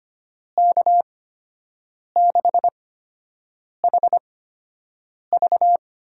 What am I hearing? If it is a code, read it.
Morse code, heard as a keyed tone, about 25 wpm, spaced as for 7 wpm, 700 Hz. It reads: K6HV
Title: USA Call Signs - 6 Characters - 8x Character Spacing - 25wpm